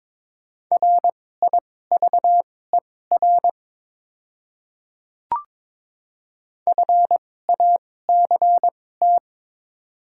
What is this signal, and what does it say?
Morse code, heard as a keyed tone, about 22 wpm, spaced as for 16 wpm, 690 Hz. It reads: RIVER FACT
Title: 2x Character Spacing - Top 500 Words - 22wpm